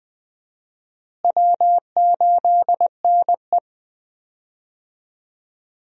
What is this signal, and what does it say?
Morse code, heard as a keyed tone, about 20 wpm, 700 Hz. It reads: W8NE
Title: USA Call Signs - 4 Characters - Repeated 3x - 20wpm